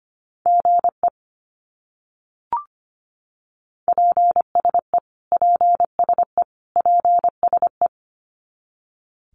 Morse code, heard as a keyed tone, about 25 wpm, 700 Hz. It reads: GE PSE PSE PSE
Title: Abbreviations Repeated 3x - 25wpm